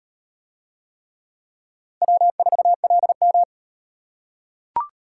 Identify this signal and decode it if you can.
Morse code, heard as a keyed tone, about 34 wpm, 700 Hz. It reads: W4LM